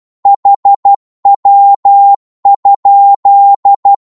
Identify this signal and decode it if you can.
Morse code, heard as a keyed tone, about 12 wpm, 800 Hz. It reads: HW?
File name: Morse / qso-12wpm